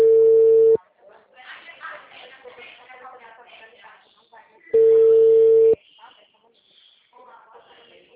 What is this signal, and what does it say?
Ringback tone (China, 450 Hz), with speech.